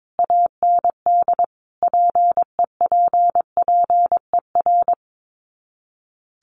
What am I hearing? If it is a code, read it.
Morse code, heard as a keyed tone, about 22 wpm, 700 Hz. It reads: AND PEPPER
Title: Sets of 5 Words 22wpm